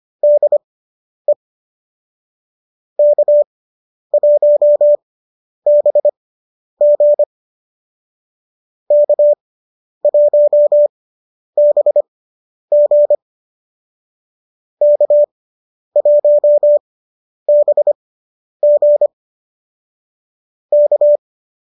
Morse code, heard as a keyed tone, about 25 wpm, spaced as for 10 wpm, 590 Hz. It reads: DE K1BG K1BG K1BG K